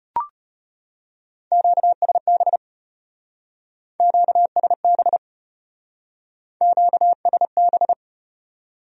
Morse code, heard as a keyed tone, about 33 wpm, 700 Hz. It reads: QSB QSB QSB